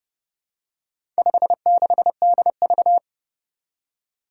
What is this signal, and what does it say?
Morse code, heard as a keyed tone, about 32 wpm, 700 Hz. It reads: 56DV
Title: Four Challenging Letter-Number Combinations 30wpm